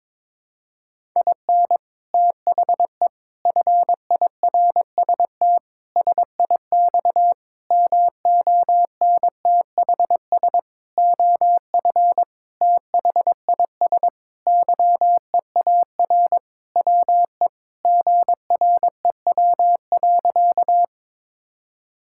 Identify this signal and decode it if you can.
Morse code, heard as a keyed tone, about 22 wpm, 700 Hz. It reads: IN THE FIRST SIX MONTHS OF THIS YEAR WE GREW.
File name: practice / 22wpm / Sentences with 8-Word Phrases - 22wpm